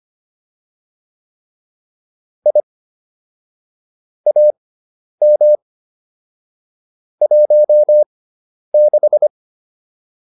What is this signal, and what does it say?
Morse code, heard as a keyed tone, about 25 wpm, spaced as for 10 wpm, 600 Hz. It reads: I AM 16